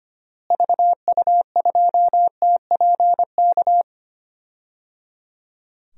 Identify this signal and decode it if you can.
Morse code, heard as a keyed tone, about 25 wpm, 700 Hz. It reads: VU2TPK
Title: DXCC-100 Call Signs - 6 Characters - Repeated 3x - 25wpm